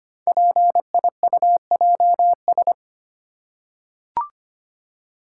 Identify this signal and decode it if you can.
Morse code, heard as a keyed tone, about 25 wpm, 700 Hz. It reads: PIUJS
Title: Code Groups - 5 Letters - 25wpm